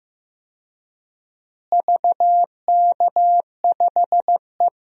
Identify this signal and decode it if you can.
Morse code, heard as a keyed tone, about 15 wpm, 700 Hz. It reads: VK5E
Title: DXCC-100 Call Signs - 4 Characters - Repeated 2x - 15wpm